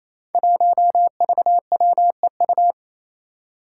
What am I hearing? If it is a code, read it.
Morse code, heard as a keyed tone, about 27 wpm, 700 Hz. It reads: 1VWEU
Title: Five Challenging Letter-Number Combinations 28wpm